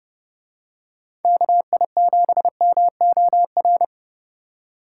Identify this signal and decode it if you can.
Morse code, heard as a keyed tone, about 28 wpm, 700 Hz. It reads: KI7MOR